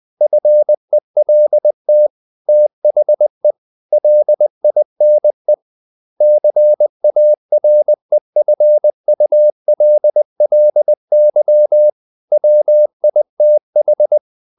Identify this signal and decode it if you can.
Morse code, heard as a keyed tone, about 20 wpm, 600 Hz. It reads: FELT THE LINE CAREFULLY WITH
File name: the old man and the sea / the old man and the sea part 1 20wpm